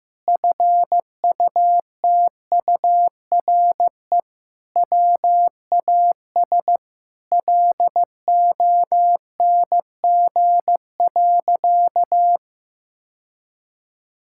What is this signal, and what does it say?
Morse code, heard as a keyed tone, about 16 wpm, 700 Hz. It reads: FUTURE WAS LONG.
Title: Sentences from Top 1000 Words - Review Part 6 - 15wpm